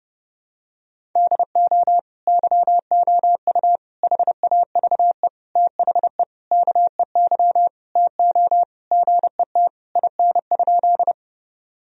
Morse code, heard as a keyed tone, about 29 wpm, 700 Hz. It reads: DO YOU HAVE THE KEY TO GET IN?
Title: Sentences from Top 1000 Words - Review Part 4 - 30wpm